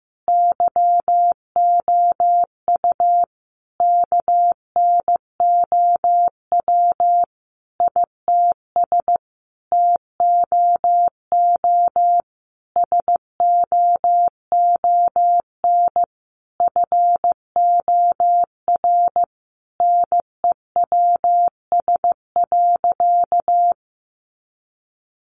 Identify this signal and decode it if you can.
Morse code, heard as a keyed tone, about 15 wpm, 700 Hz. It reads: YOU KNOW ITS TOO SOON FOR NEWS.
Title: Sentences from Top 800 Words 15wpm